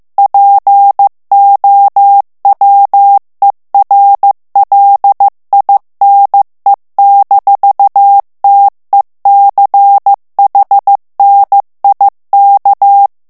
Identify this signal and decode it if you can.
Morse code, heard as a keyed tone, about 15 wpm, 800 Hz. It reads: POWERLINE-TECHNIK
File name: morse mp3 / ALLEMAND texte clair 1 vitesse 14 mots minute